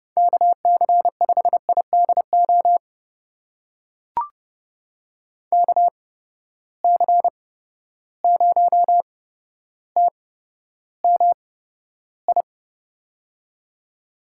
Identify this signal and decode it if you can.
Morse code, heard as a keyed tone, about 30 wpm, spaced as for 13 wpm, 700 Hz. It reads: KC5IDO KC0TMI
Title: USA Call Signs - 6 Characters - 8x Character Spacing - 30wpm